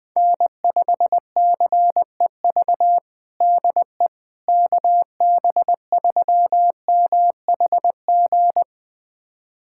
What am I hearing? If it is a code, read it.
Morse code, heard as a keyed tone, about 20 wpm, 700 Hz. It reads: N5CEV DE KB3MHG
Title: USA Callsign DE Callsign 20wpm